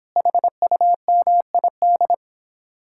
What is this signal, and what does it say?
Morse code, heard as a keyed tone, about 26 wpm, 700 Hz. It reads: HUMID